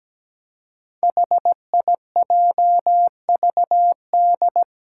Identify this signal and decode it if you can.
Morse code, heard as a keyed tone, about 17 wpm, 700 Hz. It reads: HIJVD
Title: Five Challenging Letter-Number Combinations 17wpm